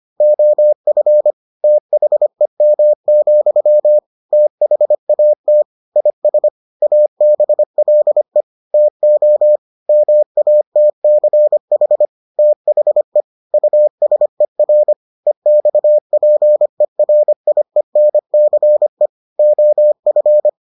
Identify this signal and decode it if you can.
Morse code, heard as a keyed tone, about 25 wpm, 600 Hz. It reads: OF THEM, THAT IS ABLE TO MATCH THE USER EXPERIENCE OF